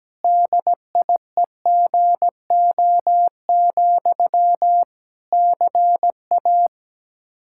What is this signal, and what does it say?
Morse code, heard as a keyed tone, about 17 wpm, 700 Hz. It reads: DIEGO, CA
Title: Practice QSO - Exchange 1 - Beginning - 17wpm